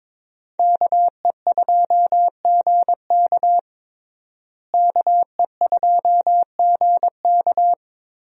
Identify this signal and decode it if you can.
Morse code, heard as a keyed tone, about 22 wpm, 700 Hz. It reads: KE2GK KE2GK
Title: USA Call Signs - 5 Characters - Repeated 3x - 22wpm